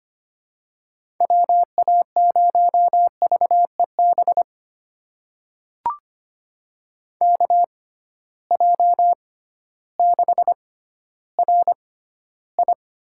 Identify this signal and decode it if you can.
Morse code, heard as a keyed tone, about 25 wpm, spaced as for 13 wpm, 700 Hz. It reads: WA0VEB KJ6RI
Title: USA Call Signs - 6 Characters - 6x Character Spacing - 25wpm